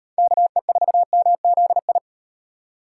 Morse code, heard as a keyed tone, about 36 wpm, 700 Hz. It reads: KE4MZI